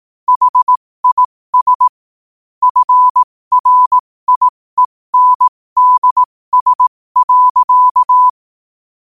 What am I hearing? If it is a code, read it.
Morse code, heard as a keyed tone, about 18 wpm, spaced as for 15 wpm, 1000 Hz. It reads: HIS FRIENDS.
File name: qotdfiles / 15wpm